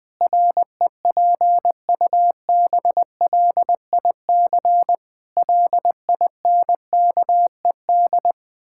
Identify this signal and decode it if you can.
Morse code, heard as a keyed tone, about 20 wpm, 700 Hz. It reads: REPUBLIC LINKED
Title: Pangram Challenge v2 - 20wpm